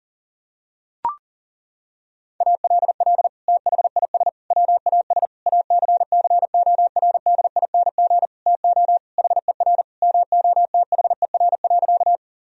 Morse code, heard as a keyed tone, about 40 wpm, 700 Hz. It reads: ALL THIS WAS ACCORDING TO HER MOTHER.
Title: Sentences from Top 1000 Words - Review Part 4 - 40wpm